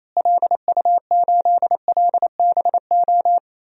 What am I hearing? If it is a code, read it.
Morse code, heard as a keyed tone, about 28 wpm, 700 Hz. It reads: LU8LBO
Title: DXCC-100 Call Signs - 6 Characters - Repeated 2x - 28wpm